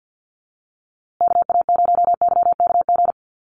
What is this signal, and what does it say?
Morse code, heard as a keyed tone, about 48 wpm, 700 Hz. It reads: KA0YKG
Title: USA Call Signs Repeated 3x 50wpm